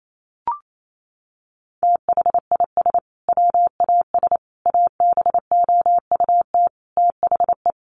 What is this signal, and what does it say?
Morse code, heard as a keyed tone, about 28 wpm, 700 Hz. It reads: THIS WAS ABOUT THE
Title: Sentences from Top 1500 Words 28wpm